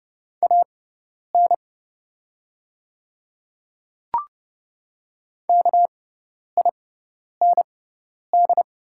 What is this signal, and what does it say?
Morse code, heard as a keyed tone, about 30 wpm, spaced as for 10 wpm, 690 Hz. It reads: AN KIND
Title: 6x Character Spacing - Top 200 Words - 30wpm